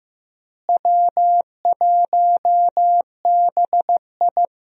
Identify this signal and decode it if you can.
Morse code, heard as a keyed tone, about 15 wpm, 700 Hz. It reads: W1BI
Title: USA Call Signs - 4 Characters - Repeated 3x - 15wpm